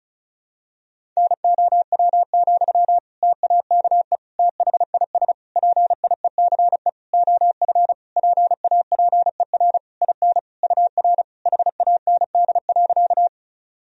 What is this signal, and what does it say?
Morse code, heard as a keyed tone, about 36 wpm, 700 Hz. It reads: NOW, TAKE THIS PIECE OF PAPER IN UR HAND.
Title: Sentences from Top 400 Words and Abbreviations 35wpm